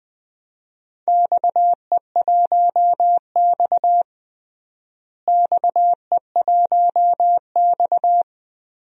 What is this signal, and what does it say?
Morse code, heard as a keyed tone, about 20 wpm, 700 Hz. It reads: XE1X XE1X
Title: DXCC-100 Call Signs - 4 Characters - Repeated 3x - 20wpm